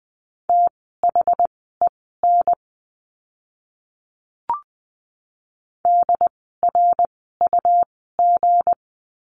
Morse code, heard as a keyed tone, about 20 wpm, spaced as for 14 wpm, 700 Hz. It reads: THEN DRUG
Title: 160 Lesson 26f - Letter g - Words 20wpm